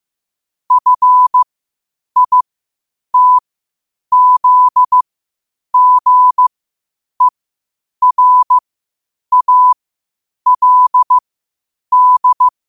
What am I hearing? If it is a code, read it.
Morse code, heard as a keyed tone, about 15 wpm, spaced as for 8 wpm, 1000 Hz. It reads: FITZGERALD
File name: 8wpm